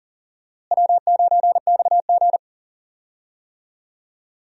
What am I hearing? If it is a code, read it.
Morse code, heard as a keyed tone, about 37 wpm, 700 Hz. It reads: W9XG
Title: USA Call Signs - 4 Characters - Repeated 2x - 40wpm